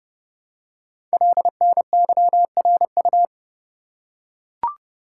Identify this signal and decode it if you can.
Morse code, heard as a keyed tone, about 28 wpm, 700 Hz. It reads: LNYRU
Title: Code Groups - 5 Letter:Numbers - 30wpm